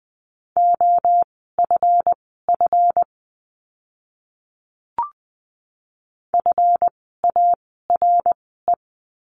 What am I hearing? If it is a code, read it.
Morse code, heard as a keyed tone, about 20 wpm, spaced as for 14 wpm, 700 Hz. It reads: OFF FARE
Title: 139 Lesson 23f - Letter f - Words 20wpm